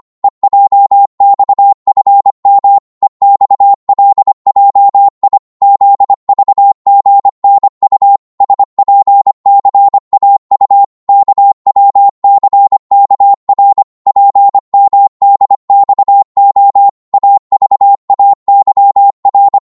Morse code, heard as a keyed tone, about 25 wpm, 800 Hz. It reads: EJXFM EXLJI ZVGNU SPCAU KWCKR PMDXO AVAYR
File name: Morse / ltr-25wpm